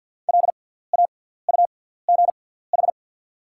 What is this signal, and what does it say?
Morse code, heard as a keyed tone, about 53 wpm, spaced as for 17 wpm, 690 Hz. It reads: LAUGH
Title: 6x Character Spacing - Top 500 Words - 50wpm